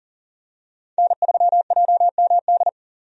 Morse code, heard as a keyed tone, about 40 wpm, 700 Hz. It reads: N3JMD
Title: USA Call Signs - 5 Characters - 40wpm